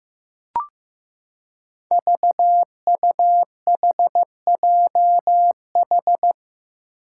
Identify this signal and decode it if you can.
Morse code, heard as a keyed tone, about 15 wpm, 700 Hz. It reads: VUHJH
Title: Five Challenging Letter-Number Combinations 15wpm